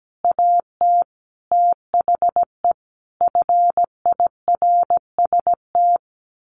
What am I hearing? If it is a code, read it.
Morse code, heard as a keyed tone, about 17 wpm, 700 Hz. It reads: AT THE FIRST
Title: A Sound Of Thunder 17wpm